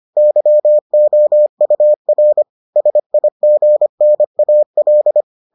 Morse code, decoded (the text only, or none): YOUR SIGNAL